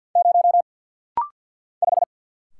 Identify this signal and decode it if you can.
Morse code, heard as a keyed tone, about 49 wpm, 690 Hz. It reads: 0 5
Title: Single Letter-Number Alt - Rapid-Fire 50wpm